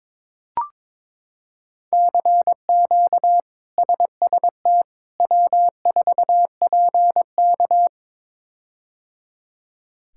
Morse code, heard as a keyed tone, about 22 wpm, 700 Hz. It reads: CQ SST W4PK
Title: K1USN SST QSO Practice - 22wpm